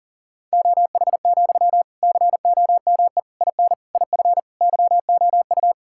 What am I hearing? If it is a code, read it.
Morse code, heard as a keyed tone, about 40 wpm, 700 Hz. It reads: OH, COME IN IF YOU
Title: Sentences from Top 1600 Words 40wpm